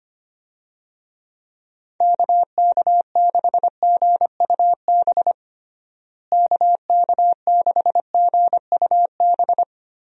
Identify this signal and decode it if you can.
Morse code, heard as a keyed tone, about 25 wpm, 700 Hz. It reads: KK6GUB KK6GUB
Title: USA Call Signs - 6 Characters - Repeated 3x - 25wpm